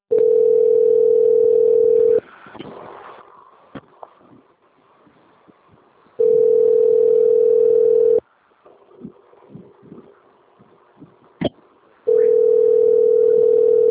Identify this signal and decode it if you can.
Ringback tone (North America), 440 + 480 Hz.
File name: out- - - - - .29704